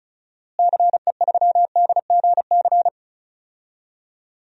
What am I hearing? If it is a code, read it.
Morse code, heard as a keyed tone, about 36 wpm, 700 Hz. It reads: CE3DGC
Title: DXCC-100 Call Signs - 6 Characters - Repeated 2x - 35wpm